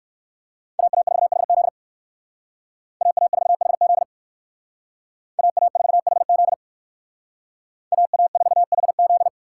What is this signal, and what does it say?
Morse code, heard as a keyed tone, about 65 wpm, 700 Hz. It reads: AA4HZ AA4HZ AA4HZ AA4HZ